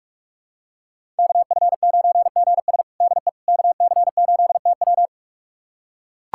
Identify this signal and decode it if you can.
Morse code, heard as a keyed tone, about 45 wpm, 700 Hz. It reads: KF9GS DE KC8TW
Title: USA Callsign DE Callsign - 5 Characters - 45wpm